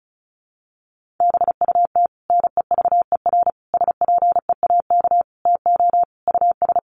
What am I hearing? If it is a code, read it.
Morse code, heard as a keyed tone, about 34 wpm, 700 Hz. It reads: BUT NEVER SPEAK TO US
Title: Sentences from Top 1100 Words 35wpm